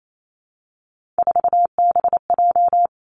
Morse code, heard as a keyed tone, about 28 wpm, 700 Hz. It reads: 4BJ
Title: Three Challenging Letter-Number Combinations 28wpm